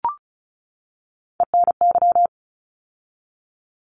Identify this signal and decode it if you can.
Morse code, heard as a keyed tone, about 34 wpm, 700 Hz. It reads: ENY
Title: ARRL Sections 35wpm